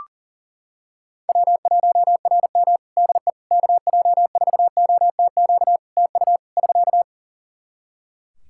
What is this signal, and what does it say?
Morse code, heard as a keyed tone, about 41 wpm, 700 Hz. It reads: W1RM DE KJ4OTQ TU <SK>
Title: Practice QSO - Exchange 4 - Ending - 40wpm